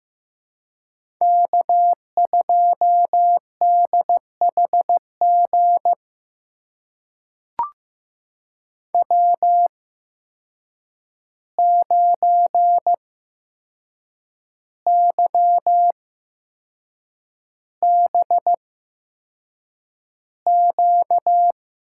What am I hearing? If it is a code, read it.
Morse code, heard as a keyed tone, about 15 wpm, spaced as for 6 wpm, 700 Hz. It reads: K2DHG W9YBQ
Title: USA Call Signs - 5 Characters - 8x Character Spacing - 15wpm